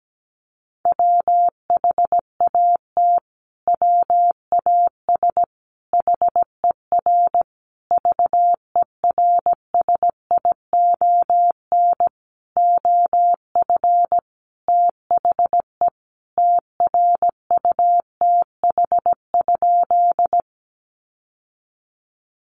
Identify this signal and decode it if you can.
Morse code, heard as a keyed tone, about 17 wpm, 700 Hz. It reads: WHAT WAS HER VERSION OF THE TRUTH?
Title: Sentences from Top 1400 Words 17wpm